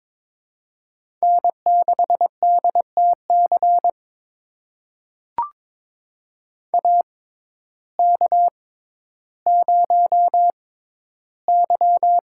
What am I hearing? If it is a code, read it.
Morse code, heard as a keyed tone, about 22 wpm, spaced as for 12 wpm, 700 Hz. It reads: N6DTC AK0Y
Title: USA Call Signs - 6x Character Spacing - 22wpm